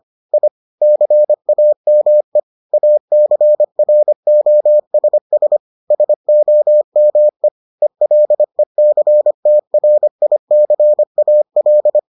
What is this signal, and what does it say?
Morse code, heard as a keyed tone, about 25 wpm, 600 Hz. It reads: I CAME ACROSS SOME ELECTRICAL